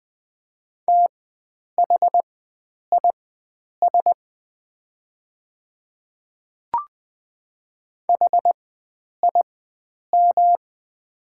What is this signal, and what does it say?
Morse code, heard as a keyed tone, about 20 wpm, spaced as for 9 wpm, 690 Hz. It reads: THIS HIM
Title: 4x Character Spacing - Top 100 Words - 20wpm